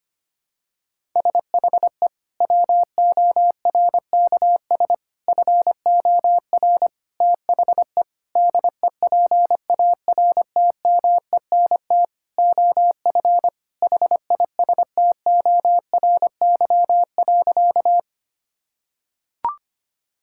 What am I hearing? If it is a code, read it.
Morse code, heard as a keyed tone, about 25 wpm, 700 Hz. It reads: SHE WORKS FOR THE DEPARTMENT OF HISTORY.
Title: Sentences from Top 1000 Words - Review Part 5 - 25wpm